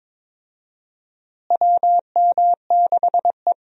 Morse code, heard as a keyed tone, about 22 wpm, 700 Hz. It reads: WM6E
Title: USA Call Signs - 4 Characters - Repeated 2x - 22wpm